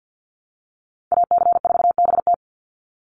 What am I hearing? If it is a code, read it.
Morse code, heard as a keyed tone, about 49 wpm, 700 Hz. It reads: AC4BT
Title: Call Signs Repeated 3x - 50wpm